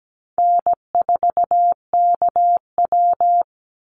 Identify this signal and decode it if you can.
Morse code, heard as a keyed tone, about 17 wpm, 700 Hz. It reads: N4KW
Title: Call Signs Repeated 3x - 17wpm